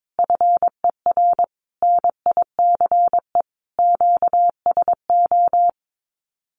Morse code, heard as a keyed tone, about 22 wpm, 700 Hz. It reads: FER NICE QSO
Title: Common QSO Phrases Encore 22wpm